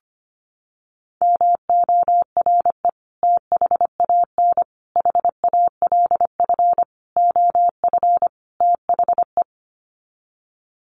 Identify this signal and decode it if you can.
Morse code, heard as a keyed tone, about 25 wpm, 700 Hz. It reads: MORE THAN HALF OF THE
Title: Sets of 5 Words 25wpm